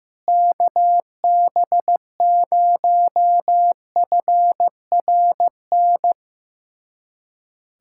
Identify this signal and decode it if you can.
Morse code, heard as a keyed tone, about 15 wpm, 700 Hz. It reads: KB0FRN
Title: USA Callsign DE Callsign - 6 Characters - 15wpm